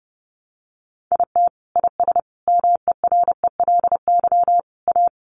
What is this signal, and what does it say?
Morse code, heard as a keyed tone, about 30 wpm, 700 Hz. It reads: IT IS MERELY A
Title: Sentences from Top 1400 Words 30wpm